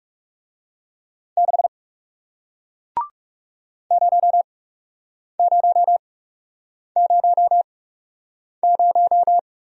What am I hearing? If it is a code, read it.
Morse code, heard as a keyed tone, about 37 wpm, spaced as for 8 wpm, 700 Hz. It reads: 6 0000
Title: Speed-Racing Single Letter-Number 30wpm